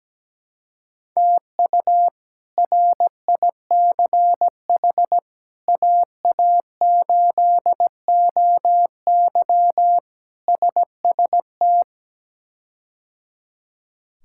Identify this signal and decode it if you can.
Morse code, heard as a keyed tone, about 17 wpm, 700 Hz. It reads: TU RICH AA8OY SST
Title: K1USN SST QSO Practice - 17wpm